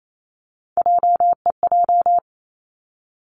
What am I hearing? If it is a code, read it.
Morse code, heard as a keyed tone, about 28 wpm, 700 Hz. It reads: JEJ
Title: Three Challenging Letter-Number Combinations 28wpm